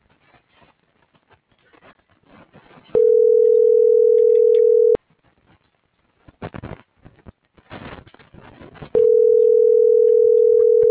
Ringback tone (North America), 440 + 480 Hz.